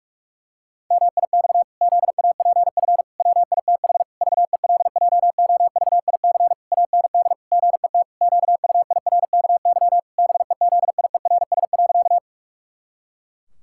Morse code, heard as a keyed tone, about 41 wpm, 700 Hz. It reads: MIX ZAPF WITH VELJOVIC AND GET QUIRKY BEZIERS.
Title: Pangram Challenge v2 - 45wpm